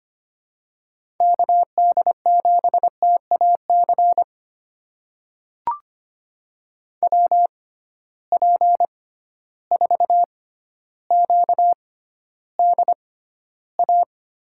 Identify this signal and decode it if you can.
Morse code, heard as a keyed tone, about 25 wpm, spaced as for 13 wpm, 700 Hz. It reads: KD7TAC WP4QDA
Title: USA Call Signs - 6 Characters - 6x Character Spacing - 25wpm